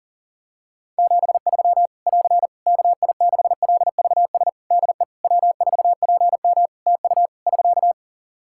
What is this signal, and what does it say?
Morse code, heard as a keyed tone, about 40 wpm, 700 Hz. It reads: 73 <AR> KI6LVS DE W4PM TU <SK>
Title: Practice QSO - Exchange 4 - Ending - 40wpm